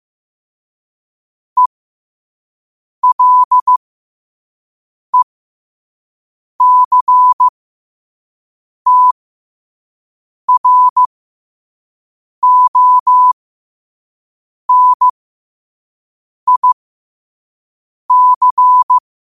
Morse code, heard as a keyed tone, about 15 wpm, spaced as for 5 wpm, 1000 Hz. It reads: ELECTRONIC
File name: qotdfiles / 5wpm